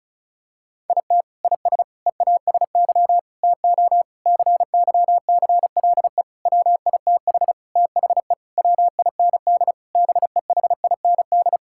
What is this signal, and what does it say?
Morse code, heard as a keyed tone, about 34 wpm, 700 Hz. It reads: IT IS EASY TO CYCLE WITH THE WIND BEHIND
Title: Sentences from Top 1000 Words - Review Part 6 - 35wpm